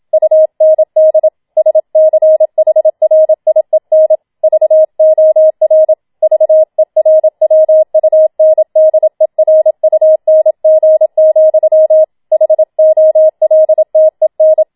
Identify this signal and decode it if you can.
Morse code, heard as a keyed tone, about 27 wpm, 630 Hz. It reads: UND SCHRIEN VOR VERWUNDERUNG, HOLTEN